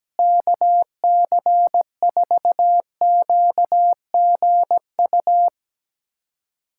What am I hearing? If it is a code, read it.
Morse code, heard as a keyed tone, about 17 wpm, 700 Hz. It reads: KC4QGU
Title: USA Callsign DE Callsign - 6 Characters - 17wpm